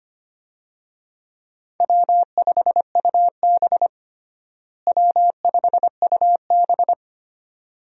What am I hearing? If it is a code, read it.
Morse code, heard as a keyed tone, about 25 wpm, 700 Hz. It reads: W5UB W5UB
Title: USA Call Signs - 4 Characters - Repeated 2x - 25wpm